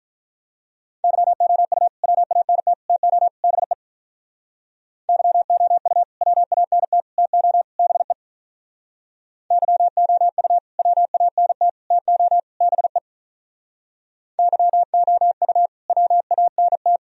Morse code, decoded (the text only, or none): YOU WANT TO BE YOU WANT TO BE YOU WANT TO BE YOU WANT